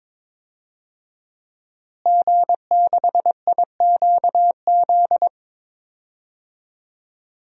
Morse code, heard as a keyed tone, about 22 wpm, 700 Hz. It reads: G6IQZ
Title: DXCC-100 Call Signs - 5 Characters - Repeated 2x - 22wpm